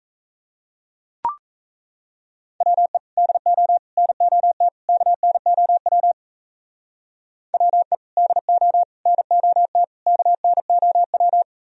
Morse code, heard as a keyed tone, about 34 wpm, 700 Hz. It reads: WE DO NOT KNOW WE DO NOT KNOW
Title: Speed-Racing - Sets of 4 Words 28wpm